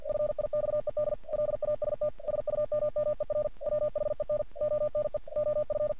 Morse code, heard as a keyed tone, about 48 wpm, 600 Hz. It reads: FIXED LAST SUMMER WHEN ONE OF
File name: archived / CRSnet